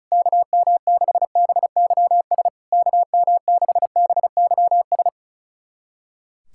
Morse code, heard as a keyed tone, about 36 wpm, 690 Hz. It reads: KM6BYS KM6BYS
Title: SOTA QSO Practice 35wpm